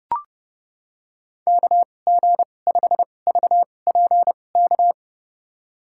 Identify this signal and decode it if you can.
Morse code, heard as a keyed tone, about 31 wpm, spaced as for 22 wpm, 700 Hz. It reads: KG5VPK
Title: USA Call Signs - 2x Character Spacing - 30wpm